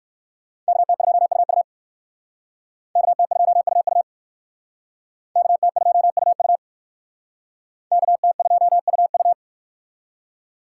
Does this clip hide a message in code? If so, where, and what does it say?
Yes: Morse code from 0.7 to 9.3 s, heard as a keyed tone, about 56 wpm, 700 Hz. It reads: KT2UU KT2UU KT2UU KT2UU